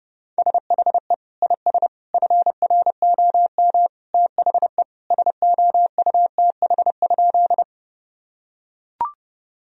Morse code, heard as a keyed tone, about 30 wpm, 700 Hz. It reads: SHE IS FROM THE SOUTH?
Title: Sentences from Top 1000 Words - Review Part 2 - 30wpm